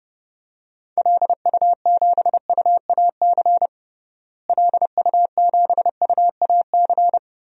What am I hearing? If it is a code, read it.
Morse code, heard as a keyed tone, about 30 wpm, 700 Hz. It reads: LU7UAC LU7UAC
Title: DXCC-100 Call Signs - 6 Characters - Repeated 3x - 30wpm